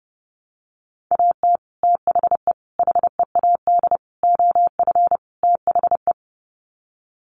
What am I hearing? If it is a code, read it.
Morse code, heard as a keyed tone, about 30 wpm, 700 Hz. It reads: AT THE HEAD OF THE
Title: Sets of 5 Words 30wpm